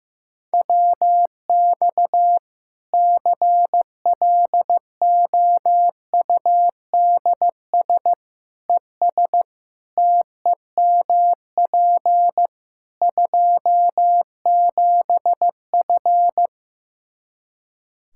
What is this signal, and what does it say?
Morse code, heard as a keyed tone, about 15 wpm, 700 Hz. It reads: WX CLOUDS ES TEMP 27F
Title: Practice QSO - Exchange 2 - Info - 15wpm